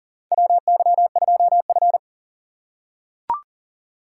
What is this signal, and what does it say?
Morse code, heard as a keyed tone, about 39 wpm, 700 Hz. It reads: WY2F
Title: USA Call Signs - 4x Character Spacing - 40wpm